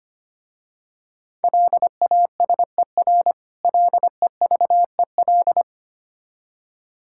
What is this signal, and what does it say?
Morse code, heard as a keyed tone, about 25 wpm, 700 Hz. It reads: LASER LEVEL
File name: practice / 25wpm / Difficult Words with Hints - 25wpm